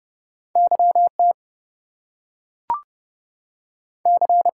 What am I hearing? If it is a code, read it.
Morse code, heard as a keyed tone, about 30 wpm, 700 Hz. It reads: YT C
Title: US States and Canadian Provinces 30wpm